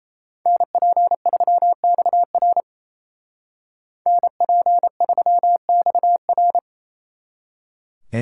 Morse code, heard as a keyed tone, about 33 wpm, 700 Hz. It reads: NP3XR NP3XR
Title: Speed Racing - USA Call Signs 28wpm